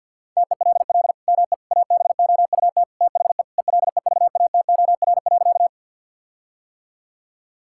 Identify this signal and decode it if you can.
Morse code, heard as a keyed tone, about 50 wpm, 700 Hz. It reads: TELL ME ABOUT THE ELEVATOR.
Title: Sentences from Top 1600 Words 50wpm